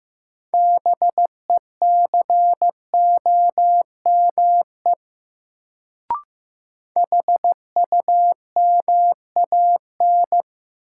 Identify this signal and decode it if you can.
Morse code, heard as a keyed tone, about 15 wpm, 700 Hz. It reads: BECOME HUMAN
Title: Sentences from Top 600 Words - New Word Drill - 15wpm